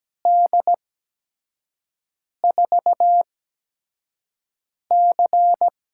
Morse code, heard as a keyed tone, about 17 wpm, spaced as for 5 wpm, 700 Hz. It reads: D4C
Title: USA Call Signs - 4 Characters - 8x Character Spacing - 17wpm